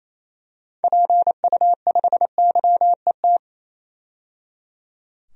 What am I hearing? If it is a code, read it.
Morse code, heard as a keyed tone, about 28 wpm, 700 Hz. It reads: PU5YET
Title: DXCC-100 Call Signs - 6 Characters - 28wpm